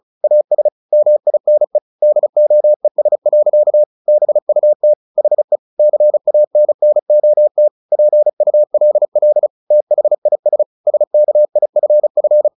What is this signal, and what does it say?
Morse code, heard as a keyed tone, about 34 wpm, 600 Hz. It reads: AS MINE DOES. BUT HE CANNOT PULL THIS SKIFF
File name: the old man and the sea / the old man and the sea 1 35wpm